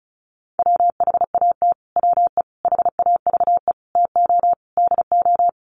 Morse code, heard as a keyed tone, about 34 wpm, 700 Hz. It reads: WHAT WE HAVE TO DO
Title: Sets of 5 Words Encore 35wpm